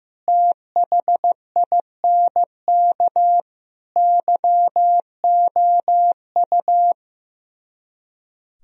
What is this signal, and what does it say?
Morse code, heard as a keyed tone, about 15 wpm, 700 Hz. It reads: THINK YOU
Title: Speed-Racing - Sets of 4 Words 15wpm